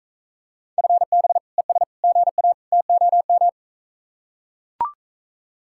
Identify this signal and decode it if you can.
Morse code, heard as a keyed tone, about 44 wpm, 700 Hz. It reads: FB ES GA TOM